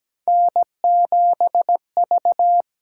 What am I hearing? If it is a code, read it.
Morse code, heard as a keyed tone, about 17 wpm, 700 Hz. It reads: N7V
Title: USA Call Signs - 8x Character Spacing - 17wpm